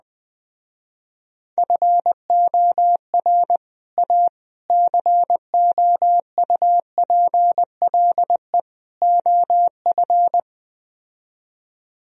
Morse code, heard as a keyed tone, about 20 wpm, 700 Hz. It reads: FOR A COUPLE OF